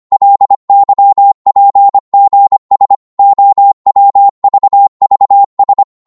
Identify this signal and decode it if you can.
Morse code, heard as a keyed tone, about 25 wpm, 800 Hz. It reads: LYPGS OWVVS
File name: Morse / ltr-25wpm